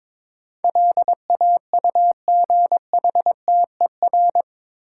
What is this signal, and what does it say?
Morse code, heard as a keyed tone, about 22 wpm, 700 Hz. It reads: LAUGHTER
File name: practice / 22wpm / Eight Letter Words - 22wpm